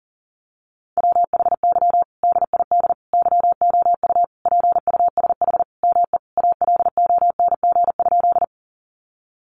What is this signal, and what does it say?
Morse code, heard as a keyed tone, about 40 wpm, 700 Hz. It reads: WHY DID YOU PUSH ME ALONG?
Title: Sentences from Top 1300 Words 40wpm